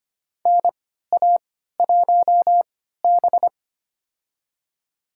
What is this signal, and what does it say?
Morse code, heard as a keyed tone, about 25 wpm, spaced as for 14 wpm, 700 Hz. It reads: NA1B
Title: USA Call Signs - 4 Characters - 3x Character Spacing - 25wpm